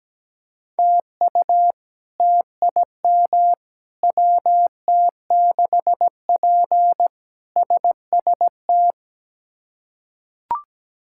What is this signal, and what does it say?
Morse code, heard as a keyed tone, about 17 wpm, 700 Hz. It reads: TU TIM WT6P SST
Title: K1USN SST QSO Practice - 17wpm